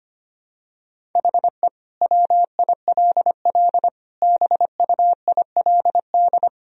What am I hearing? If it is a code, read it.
Morse code, heard as a keyed tone, about 25 wpm, 700 Hz. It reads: HE WILL BUILD
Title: Sentences from Top 1000 Words - Review Part 1 - 25wpm